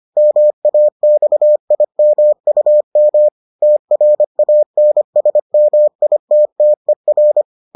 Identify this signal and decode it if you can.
Morse code, heard as a keyed tone, about 25 wpm, 600 Hz. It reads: MAXIMUM TRANSMITTER